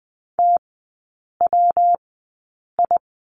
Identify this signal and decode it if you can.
Morse code, heard as a keyed tone, about 20 wpm, spaced as for 8 wpm, 700 Hz. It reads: TWI
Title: 114 Lesson 19e - Letter w - Three Characters 20wpm